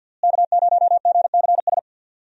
Morse code, heard as a keyed tone, about 47 wpm, 700 Hz. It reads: K0GCS
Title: USA Call Signs - 5 Characters - Repeated 2x - 50wpm